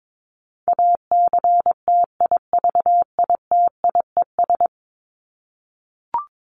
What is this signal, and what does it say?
Morse code, heard as a keyed tone, about 22 wpm, 700 Hz. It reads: ACTIVITIES
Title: ES:S Suffix - Top 100 Words with Base Words 22wpm